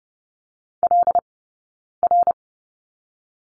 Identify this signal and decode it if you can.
Morse code, heard as a keyed tone, about 30 wpm, 700 Hz. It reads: L R
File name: course / 30:10wpm / 086 Lesson 15d - Number 5 - Two Characters 30wpm